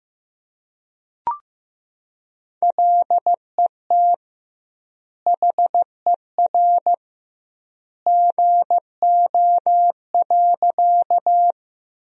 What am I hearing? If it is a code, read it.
Morse code, heard as a keyed tone, about 16 wpm, 700 Hz. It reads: LET HER GO.
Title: 2x Word-Spacing - Sentences from Top 300 Words - 15wpm